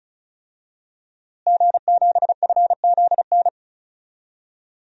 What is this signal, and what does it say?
Morse code, heard as a keyed tone, about 34 wpm, 700 Hz. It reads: G7FZN
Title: DXCC-100 Call Signs - 5 Characters - Repeated 2x - 35wpm